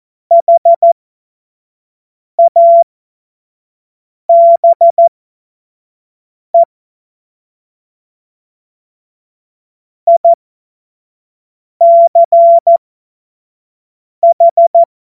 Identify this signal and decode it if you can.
Morse code, heard as a keyed tone, about 14 wpm, spaced as for 5 wpm, 680 Hz. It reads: HABE ICH
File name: Morsekurs / Witze / Witz 12 Dezember 2023 25 F70